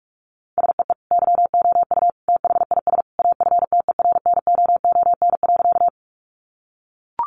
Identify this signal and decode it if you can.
Morse code, heard as a keyed tone, about 44 wpm, 700 Hz. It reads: SEE YOU THIS AFTERNOON.
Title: Sentences from Top 1500 Words 45wpm